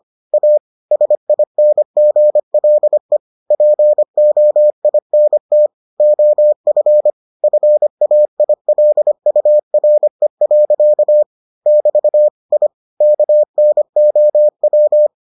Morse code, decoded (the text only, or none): A SINGLE POINT OF FAILURE. <BT> I KNOW